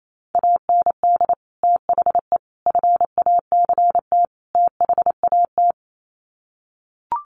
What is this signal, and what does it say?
Morse code, heard as a keyed tone, about 28 wpm, 700 Hz. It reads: AND THE FACT THAT
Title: Sets of 4 Words 28wpm